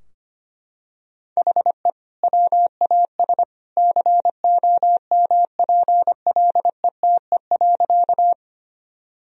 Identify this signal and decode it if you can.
Morse code, heard as a keyed tone, about 25 wpm, 700 Hz. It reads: HE WAS COMPLETE.
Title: Sentences from Top 1000 Words - Review Part 2 - 25wpm